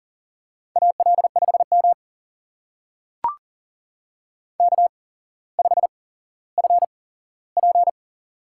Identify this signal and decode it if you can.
Morse code, heard as a keyed tone, about 39 wpm, spaced as for 17 wpm, 700 Hz. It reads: AL5M K5FP
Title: USA Call Signs - 8x Character Spacing - 40wpm